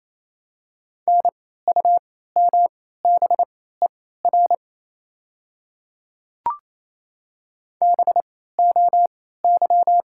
Morse code, heard as a keyed tone, about 28 wpm, spaced as for 16 wpm, 700 Hz. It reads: NUMBER BOY
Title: 3x Character Spacing - Top 200 Words - 28wpm